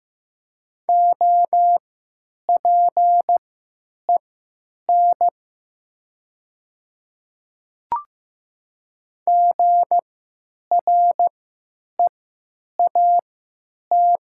Morse code, heard as a keyed tone, about 15 wpm, spaced as for 9 wpm, 700 Hz. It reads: OPEN GREAT
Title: 3x Character Spacing - Top 300 Words - 15wpm